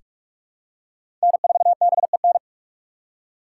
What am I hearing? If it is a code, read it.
Morse code, heard as a keyed tone, about 42 wpm, 700 Hz. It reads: N4BEN
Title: USA Call Signs - 5 Characters - 3x Character Spacing - 45wpm